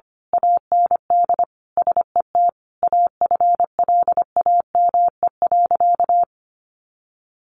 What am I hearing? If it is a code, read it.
Morse code, heard as a keyed tone, about 25 wpm, 700 Hz. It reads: AND SET AFLAME.
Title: A Sound Of Thunder 25wpm